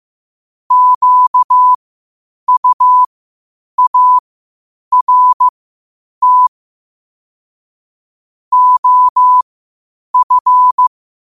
Morse code, heard as a keyed tone, about 15 wpm, spaced as for 8 wpm, 1000 Hz. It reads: QUART OF
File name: qotdfiles / 8wpm